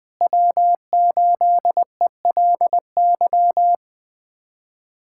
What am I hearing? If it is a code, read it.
Morse code, heard as a keyed tone, about 20 wpm, 700 Hz. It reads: W8ELY
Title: USA Callsign DE Callsign - 5 Characters - 20wpm